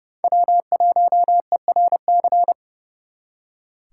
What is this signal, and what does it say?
Morse code, heard as a keyed tone, about 34 wpm, 700 Hz. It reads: W1ERC
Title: Speed Racing - USA 1x3 Call Signs 30wpm